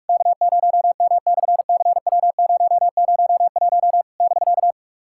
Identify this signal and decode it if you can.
Morse code, heard as a keyed tone, about 44 wpm, 660 Hz. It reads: K0M/CW001 <BK>
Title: SOTA QSO Practice 45wpm